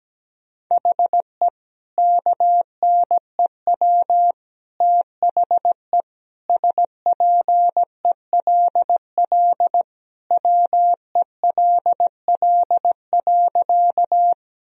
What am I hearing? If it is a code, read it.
Morse code, heard as a keyed tone, about 17 wpm, 700 Hz. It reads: HE KNEW THE SPELL WELL.
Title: Sentences from Top 1000 Words - Review Part 1 - 17wpm